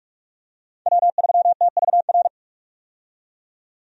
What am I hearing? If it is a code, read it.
Morse code, heard as a keyed tone, about 44 wpm, 700 Hz. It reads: W3TVR
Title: USA Call Signs - 5 Characters - 45wpm